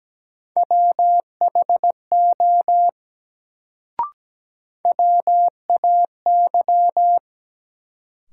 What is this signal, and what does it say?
Morse code, heard as a keyed tone, about 17 wpm, 700 Hz. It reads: WHO WAY
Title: Sentences from Top 100 Words - New Word Drill - 17wpm